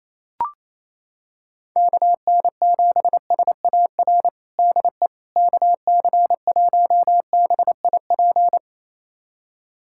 Morse code, heard as a keyed tone, about 28 wpm, 700 Hz. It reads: KN7SAR DE KC1BIP
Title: USA Callsign DE Callsign 28wpm